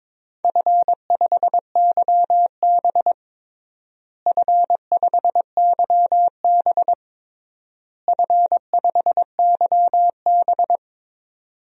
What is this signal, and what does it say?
Morse code, heard as a keyed tone, about 22 wpm, 700 Hz. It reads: F5YB F5YB F5YB
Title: DXCC-100 Call Signs - 4 Characters - Repeated 3x - 22wpm